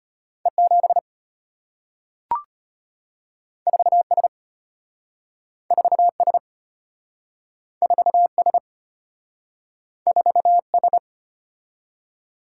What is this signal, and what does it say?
Morse code, heard as a keyed tone, about 34 wpm, 690 Hz. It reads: E7 4S 4S 4S 4S